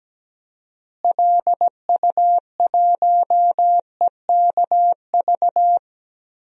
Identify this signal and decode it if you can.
Morse code, heard as a keyed tone, about 17 wpm, 700 Hz. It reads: LU1EKV
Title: DXCC-100 Call Signs - 6 Characters - Repeated 3x - 17wpm